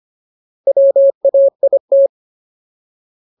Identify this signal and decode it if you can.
Morse code, heard as a keyed tone, about 25 wpm, 550 Hz. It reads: WAIT
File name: Ultimate Word List / 25WPM Ultimate